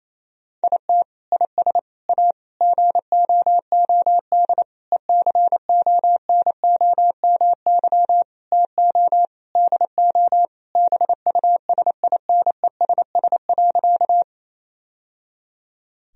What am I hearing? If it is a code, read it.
Morse code, heard as a keyed tone, about 28 wpm, 700 Hz. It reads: IT IS A GOOD ECONOMY TO DO BUSINESS.
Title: Sentences from Top 1000 Words - Review Part 3 - 28wpm